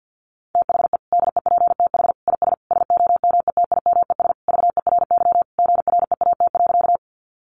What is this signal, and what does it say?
Morse code, heard as a keyed tone, about 50 wpm, 700 Hz. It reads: THE DEPTH IS SOMETIMES VERY GREAT.
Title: Sentences from Top 1200 Words 50wpm